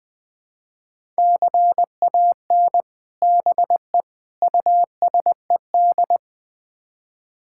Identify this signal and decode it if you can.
Morse code, heard as a keyed tone, about 20 wpm, 700 Hz. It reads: CAN BE USED
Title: Build A Phrase - 4-Word Phrases - 20wpm